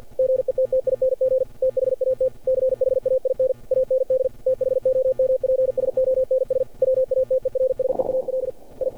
Morse code, heard as a keyed tone, about 47 wpm, 520 Hz. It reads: GETTING THAT CHAIN AND THOMPSONS WATERSEAL I
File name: archived / CRSnet